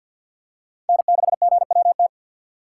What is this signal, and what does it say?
Morse code, heard as a keyed tone, about 51 wpm, 690 Hz. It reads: N6GWT
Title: USA Call Signs - 5 Characters - 4x Character Spacing - 50wpm